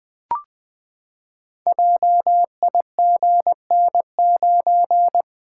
Morse code, heard as a keyed tone, about 20 wpm, 700 Hz. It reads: JIGN9
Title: Code Groups - 5 Letter:Numbers - 20wpm